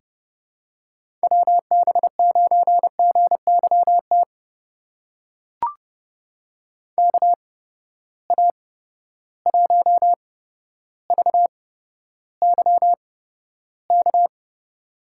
Morse code, heard as a keyed tone, about 32 wpm, spaced as for 13 wpm, 700 Hz. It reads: WB9GYT KA1VYK
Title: USA Call Signs - 6 Characters - 8x Character Spacing - 30wpm